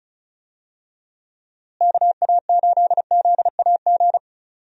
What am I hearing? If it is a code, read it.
Morse code, heard as a keyed tone, about 36 wpm, 700 Hz. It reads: KA8ZAG
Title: USA Call Signs - 6 Characters - Repeated 2x - 35wpm